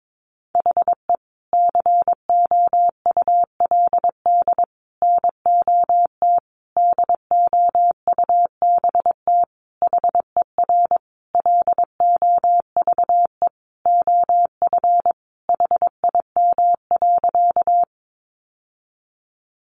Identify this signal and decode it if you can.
Morse code, heard as a keyed tone, about 22 wpm, 700 Hz. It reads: HE COULD NOT DOUBT HER LOVE OF HIM.
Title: Sentences from Top 1500 Words 22wpm